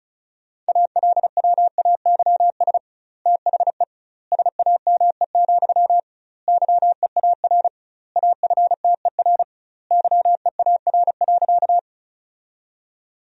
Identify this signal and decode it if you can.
Morse code, heard as a keyed tone, about 34 wpm, 700 Hz. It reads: ALWAYS THE SAME, YEAR AFTER YEAR.
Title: 2x Word-Spacing - Sentences from Top 300 Words - 35wpm